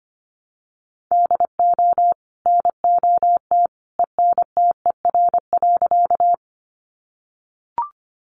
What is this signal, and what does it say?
Morse code, heard as a keyed tone, about 24 wpm, 700 Hz. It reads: DO NOT ENTER.
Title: Sentences from Top 1300 Words 25wpm